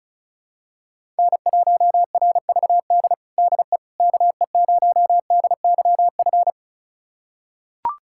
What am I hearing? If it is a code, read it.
Morse code, heard as a keyed tone, about 36 wpm, 700 Hz. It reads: N1RVD DE KE0DYF
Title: USA Callsign DE Callsign 35wpm